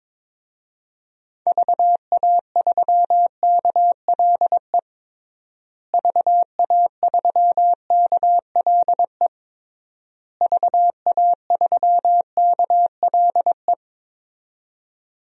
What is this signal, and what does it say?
Morse code, heard as a keyed tone, about 21 wpm, 700 Hz. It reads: VA3KLE VA3KLE VA3KLE
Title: DXCC-100 Call Signs - 6 Characters - Repeated 3x - 22wpm